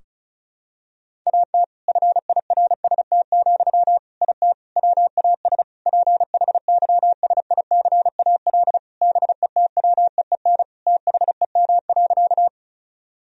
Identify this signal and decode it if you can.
Morse code, heard as a keyed tone, about 36 wpm, 700 Hz. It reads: AT FIRST, IT WAS PHYSICAL BETWEEN THEM.
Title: Sentences from Top 1000 Words - Review Part 4 - 35wpm